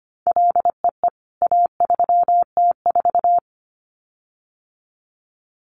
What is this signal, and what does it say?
Morse code, heard as a keyed tone, about 25 wpm, 700 Hz. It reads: LEE A3T4
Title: CWT Exchange 25wpm